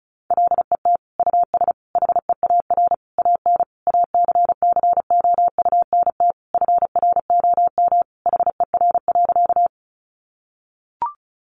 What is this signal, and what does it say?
Morse code, heard as a keyed tone, about 34 wpm, 700 Hz. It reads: LET US HEAR AN ACCOUNT FROM HER.
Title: Sentences from Top 900 Words 35wpm